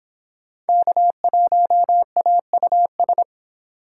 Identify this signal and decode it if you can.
Morse code, heard as a keyed tone, about 26 wpm, 700 Hz. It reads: K1AUS